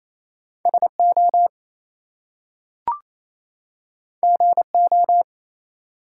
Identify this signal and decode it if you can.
Morse code, heard as a keyed tone, about 28 wpm, 700 Hz. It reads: SO GO
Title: Two Letter Words - 28wpm